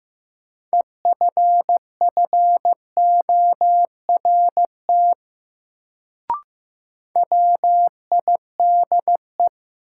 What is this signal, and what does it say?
Morse code, heard as a keyed tone, about 15 wpm, 700 Hz. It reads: EFFORT WIDE
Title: Sets of 4 Words - New Word Drill - 15wpm